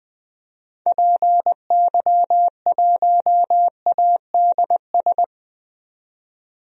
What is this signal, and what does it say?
Morse code, heard as a keyed tone, about 20 wpm, 700 Hz. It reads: PY1ADS
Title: DXCC-100 Call Signs - 6 Characters - 20wpm